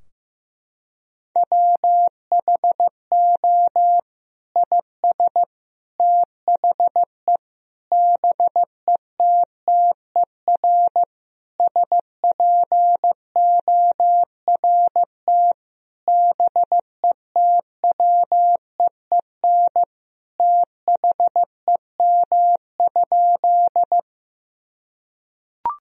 Morse code, heard as a keyed tone, about 16 wpm, 700 Hz. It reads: WHO IS THE BETTER SPORT BETWEEN THEM?
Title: Sentences from Top 1000 Words - Review Part 4 - 15wpm